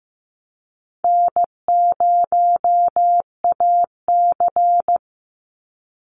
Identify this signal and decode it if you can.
Morse code, heard as a keyed tone, about 15 wpm, 700 Hz. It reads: N0AC
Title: Easy Call Signs Repeated 2x - 15wpm